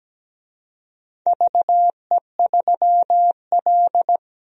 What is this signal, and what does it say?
Morse code, heard as a keyed tone, about 17 wpm, 700 Hz. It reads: VE3L
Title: DXCC-100 Call Signs - 4 Characters - Repeated 3x - 17wpm